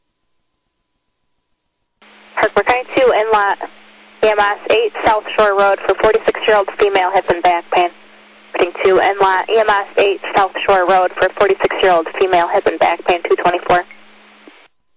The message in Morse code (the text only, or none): none